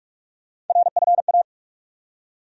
Morse code, heard as a keyed tone, about 43 wpm, 700 Hz. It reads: RFA
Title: USA Call Sign Postfixes - 3 Letters 45wpm